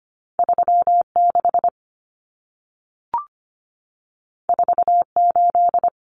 Morse code, heard as a keyed tone, about 25 wpm, 700 Hz. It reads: 36 48
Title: Common Cut Numbers - 2-Digit 25wpm